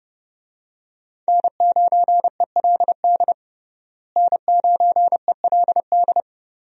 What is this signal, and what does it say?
Morse code, heard as a keyed tone, about 30 wpm, 700 Hz. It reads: N9ELD N9ELD
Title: USA Call Signs - 5 Characters - Repeated 2x - 30wpm